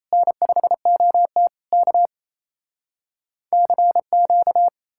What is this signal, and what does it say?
Morse code, heard as a keyed tone, about 38 wpm, 700 Hz. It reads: N5OT K CQ